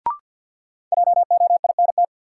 Morse code, heard as a keyed tone, about 53 wpm, 700 Hz. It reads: JOINT
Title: Sentences from Top 1300 Words - New Word Drill - 50wpm